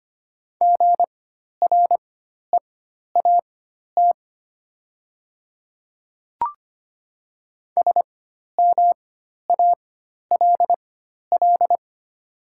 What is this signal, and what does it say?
Morse code, heard as a keyed tone, about 25 wpm, spaced as for 12 wpm, 700 Hz. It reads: GREAT SMALL
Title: 4x Character Spacing - Top 200 Words - 25wpm